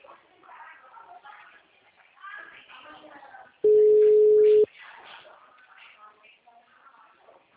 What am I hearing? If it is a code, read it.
Ringback tone (Europe, 425 Hz), with speech.